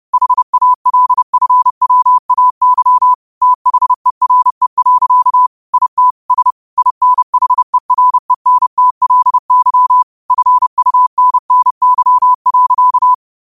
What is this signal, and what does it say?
Morse code, heard as a keyed tone, about 30 wpm, 1000 Hz. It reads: HALFWAY THERE. IT S INHERENTLY FUNNY.
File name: qotdfiles / 30wpm